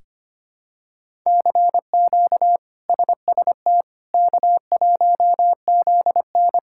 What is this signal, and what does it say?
Morse code, heard as a keyed tone, about 25 wpm, 700 Hz. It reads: CQ SST K1ZN
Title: K1USN SST QSO Practice - 25wpm